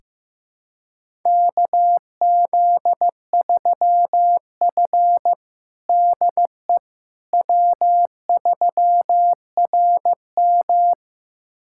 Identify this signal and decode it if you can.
Morse code, heard as a keyed tone, about 15 wpm, 700 Hz. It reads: KZ3F DE W3RM
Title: USA Callsign DE Callsign - 4 Characters - 15wpm